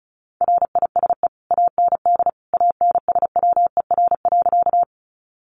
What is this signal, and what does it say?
Morse code, heard as a keyed tone, about 34 wpm, 700 Hz. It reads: RISE AND ANSWER.
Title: Sentences from Top 900 Words 35wpm